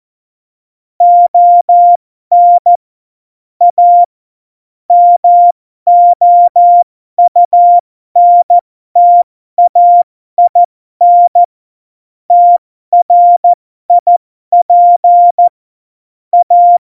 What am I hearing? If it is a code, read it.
Morse code, heard as a keyed tone, about 14 wpm, spaced as for 12 wpm, 700 Hz. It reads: ON A MOUNTAIN TRIP A
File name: Morsekurs / Witze